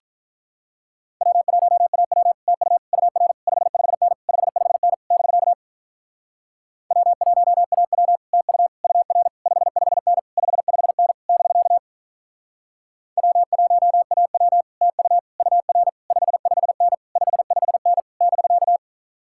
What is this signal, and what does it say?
Morse code, heard as a keyed tone, about 47 wpm, 700 Hz. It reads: W1AW TU UR 55N 55N <BK> W1AW TU UR 55N 55N <BK> W1AW TU UR 55N 55N <BK>